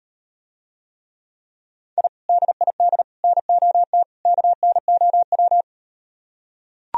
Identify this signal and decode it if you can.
Morse code, heard as a keyed tone, about 38 wpm, 700 Hz. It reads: I DID NOT KNOW